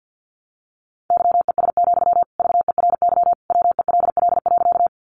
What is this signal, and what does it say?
Morse code, heard as a keyed tone, about 50 wpm, 700 Hz. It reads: YES, VERY WELL.
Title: Sentences from Top 500 Words 50wpm